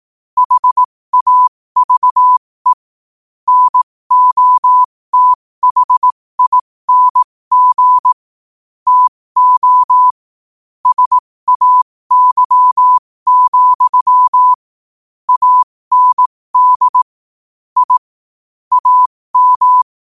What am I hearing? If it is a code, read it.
Morse code, heard as a keyed tone, about 18 wpm, spaced as for 15 wpm, 1000 Hz. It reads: HAVE NOTHING TO SAY, AND I AM
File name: qotdfiles / 15wpm